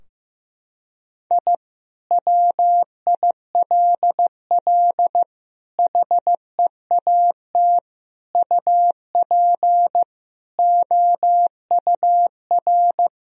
Morse code, heard as a keyed tone, about 15 wpm, 700 Hz. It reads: I WILL HEAT UP OUR
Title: Sentences from Top 1000 Words - Review Part 3 - 15wpm